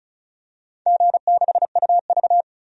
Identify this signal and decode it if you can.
Morse code, heard as a keyed tone, about 36 wpm, 700 Hz. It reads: G6UV
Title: DXCC-100 Call Signs - 4 Characters - 35wpm